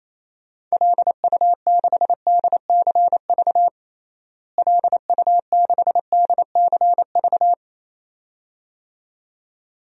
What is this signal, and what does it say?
Morse code, heard as a keyed tone, about 28 wpm, 700 Hz. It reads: LU6DCV LU6DCV
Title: DXCC-100 Call Signs - 6 Characters - Repeated 2x - 28wpm